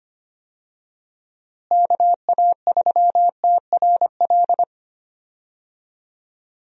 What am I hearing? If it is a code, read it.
Morse code, heard as a keyed tone, about 25 wpm, 700 Hz. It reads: KA3TRL
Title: USA Call Signs - 6 Characters - Repeated 3x - 25wpm